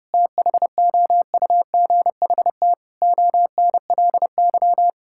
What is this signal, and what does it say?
Morse code, heard as a keyed tone, about 32 wpm, 700 Hz. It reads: THOUGHT ONLY
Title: Sentences from Top 1600 Words 30wpm